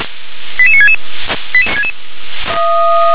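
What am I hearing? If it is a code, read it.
Five-tone selective call (ZVEI), heard as quick runs of five tones, tones of about 80 ms. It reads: 88966
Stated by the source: ZVEI five-tone selective call 88965, received on an SDR.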